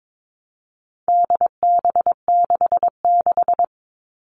Morse code, heard as a keyed tone, about 22 wpm, 700 Hz. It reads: DB66
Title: All Cut Numbers - 4-Digit 22wpm